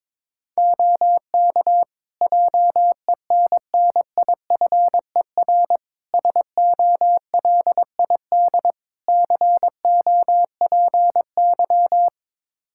Morse code, heard as a keyed tone, about 22 wpm, 700 Hz. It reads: OK JENNIFER SOLID COPY
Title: Practice QSO - Exchange 3 - More Info - 22wpm